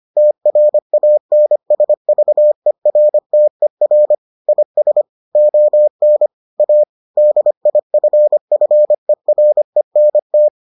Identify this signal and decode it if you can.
Morse code, heard as a keyed tone, about 25 wpm, 600 Hz. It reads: TRANSVERTER IS ON A DIFFERENT